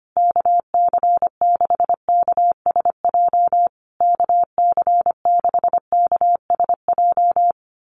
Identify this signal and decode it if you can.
Morse code, heard as a keyed tone, about 25 wpm, 700 Hz. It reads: KC6KSJ KC6KSJ
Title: USA Call Signs Repeated 3x 25wpm